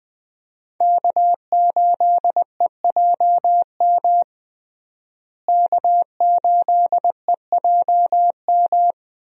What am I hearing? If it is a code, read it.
Morse code, heard as a keyed tone, about 20 wpm, 700 Hz. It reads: K8EJM K8EJM
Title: USA Call Signs - 5 Characters - Repeated 2x - 20wpm